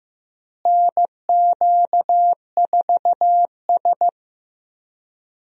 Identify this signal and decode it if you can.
Morse code, heard as a keyed tone, about 15 wpm, 700 Hz. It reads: NQ4S
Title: USA Call Signs - 4 Characters - Repeated 2x - 15wpm